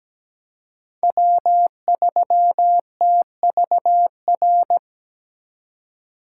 Morse code, heard as a keyed tone, about 17 wpm, 700 Hz. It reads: W3TVR
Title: USA Call Signs - 5 Characters - 2x Character Spacing - 17wpm